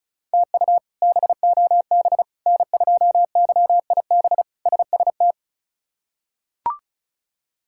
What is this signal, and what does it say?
Morse code, heard as a keyed tone, about 34 wpm, 700 Hz. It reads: TU BOB N2YIB SST
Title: K1USN SST QSO Practice - 35wpm